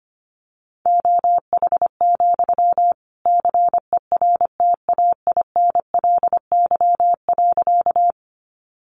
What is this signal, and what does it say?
Morse code, heard as a keyed tone, about 25 wpm, 700 Hz. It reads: OH, CERTAINLY.
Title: Sentences from Top 900 Words 25wpm